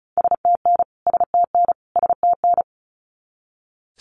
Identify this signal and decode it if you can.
Morse code, heard as a keyed tone, about 34 wpm, 700 Hz. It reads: STN STN STN
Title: Abbreviations Repeated 3x - 35wpm